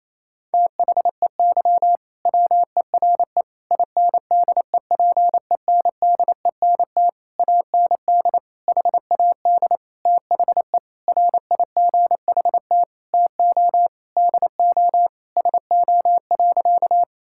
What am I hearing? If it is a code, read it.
Morse code, heard as a keyed tone, about 28 wpm, 700 Hz. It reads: THEY WERE INDEPENDENT AND HAD THE RIGHT TO DO SO.
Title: Sentences from Top 1000 Words - Review Part 6 - 28wpm